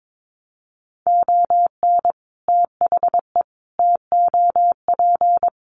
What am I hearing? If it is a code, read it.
Morse code, heard as a keyed tone, about 22 wpm, 700 Hz. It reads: ON THE TOP
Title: Sets of 5 Words Encore 22wpm